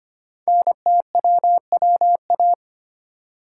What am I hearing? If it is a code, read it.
Morse code, heard as a keyed tone, about 25 wpm, 700 Hz. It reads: NTWWA
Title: Code Groups - 5 Letters - 25wpm